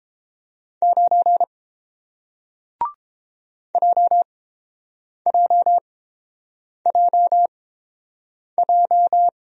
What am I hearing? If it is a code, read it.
Morse code, heard as a keyed tone, about 29 wpm, spaced as for 7 wpm, 700 Hz. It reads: 9 JJJJ